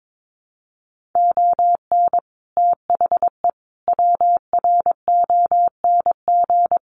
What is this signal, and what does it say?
Morse code, heard as a keyed tone, about 22 wpm, 700 Hz. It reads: ON THE WRONG
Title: Sets of 5 Words Encore 22wpm